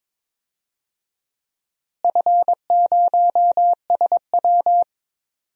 Morse code, heard as a keyed tone, about 22 wpm, 700 Hz. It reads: F0SW
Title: DXCC-100 Call Signs - 4 Characters - Repeated 3x - 22wpm